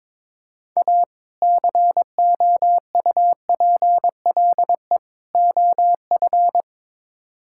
Morse code, heard as a keyed tone, about 22 wpm, 700 Hz. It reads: A COUPLE OF
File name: practice / 22wpm / Build A Phrase - 5-Word Phrases - 22wpm